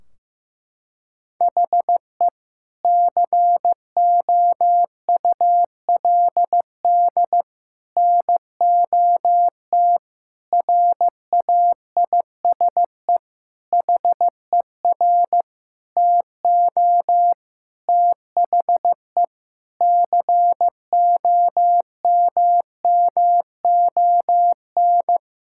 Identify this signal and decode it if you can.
Morse code, heard as a keyed tone, about 15 wpm, 700 Hz. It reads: HE COULD NOT RAISE HER TO THE COMMON
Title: Sentences from Top 1600 Words 15wpm